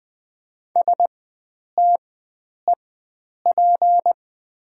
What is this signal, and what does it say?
Morse code, heard as a keyed tone, about 20 wpm, spaced as for 9 wpm, 700 Hz. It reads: STEP
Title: 4x Character Spacing - Top 500 Words - 20wpm